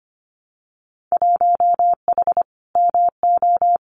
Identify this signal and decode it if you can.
Morse code, heard as a keyed tone, about 25 wpm, 700 Hz. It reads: 1H MO
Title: Winter Field Day Exchanges 25wpm